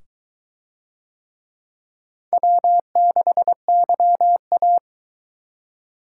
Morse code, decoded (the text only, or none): W6YA